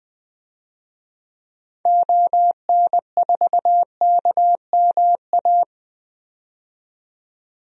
Morse code, heard as a keyed tone, about 20 wpm, 700 Hz. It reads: ON4KMA
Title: DXCC-100 Call Signs - 6 Characters - 20wpm